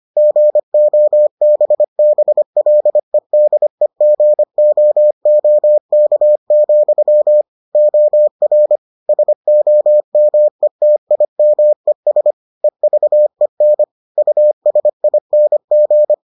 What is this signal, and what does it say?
Morse code, heard as a keyed tone, about 25 wpm, 600 Hz. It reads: GOBBLEDEGOOK, OR SOMETIMES EVEN USING